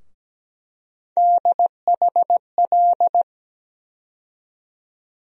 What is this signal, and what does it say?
Morse code, heard as a keyed tone, about 17 wpm, 700 Hz. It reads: DHL
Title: USA Call Sign Postfixes - 3 Letters 17wpm